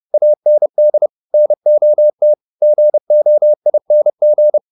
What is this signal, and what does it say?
Morse code, heard as a keyed tone, about 30 wpm, 600 Hz. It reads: AND NOT GOING
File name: the old man and the sea / the old man and the sea 1 30wpm